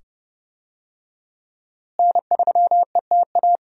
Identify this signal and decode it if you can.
Morse code, heard as a keyed tone, about 31 wpm, 700 Hz. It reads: N3ETA
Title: USA Call Signs - 5 Characters - Repeated 2x - 30wpm